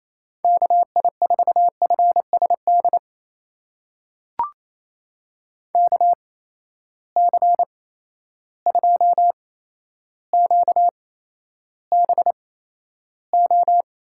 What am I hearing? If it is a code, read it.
Morse code, heard as a keyed tone, about 28 wpm, spaced as for 12 wpm, 700 Hz. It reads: KI4FSD KC2QBO
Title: USA Call Signs - 6 Characters - 8x Character Spacing - 28wpm